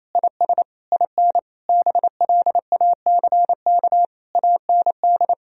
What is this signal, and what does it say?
Morse code, heard as a keyed tone, about 28 wpm, 700 Hz. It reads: IS IN BLACK AND
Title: Sentences from Top 1000 Words - Review Part 2 - 28wpm